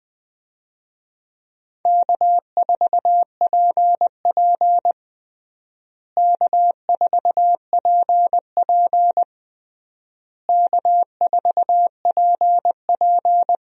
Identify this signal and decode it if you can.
Morse code, heard as a keyed tone, about 20 wpm, 700 Hz. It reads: K4PP K4PP K4PP
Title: USA Call Signs - 4 Characters - Repeated 3x - 20wpm